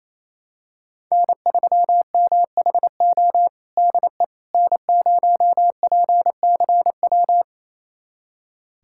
Morse code, heard as a keyed tone, about 28 wpm, 700 Hz. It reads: N3MHO DE N0PCW
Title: USA Callsign DE Callsign - 5 Characters - 28wpm